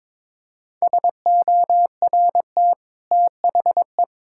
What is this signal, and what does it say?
Morse code, heard as a keyed tone, about 22 wpm, 700 Hz. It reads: SORT THE
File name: practice / 22wpm / Easy Sentences with Hints - 22wpm